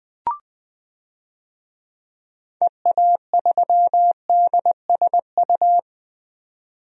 Morse code, heard as a keyed tone, about 21 wpm, 700 Hz. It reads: EA3DSU
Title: DXCC-100 Call Signs - 6 Characters - 20wpm